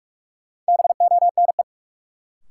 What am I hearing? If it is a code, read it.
Morse code, heard as a keyed tone, about 46 wpm, 700 Hz. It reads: BONE
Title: Sentences from Top 1600 Words - New Word Drill - 45wpm